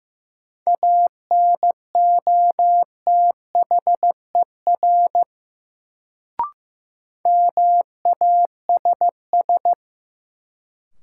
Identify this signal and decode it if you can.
Morse code, heard as a keyed tone, about 15 wpm, 700 Hz. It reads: ANOTHER MASS
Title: Sets of 5 Words Encore - New Word Drill - 15wpm